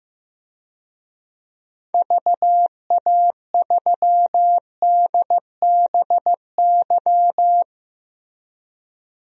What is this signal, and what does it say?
Morse code, heard as a keyed tone, about 15 wpm, 690 Hz. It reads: VA3DBY
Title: DXCC-100 Call Signs - 6 Characters - 15wpm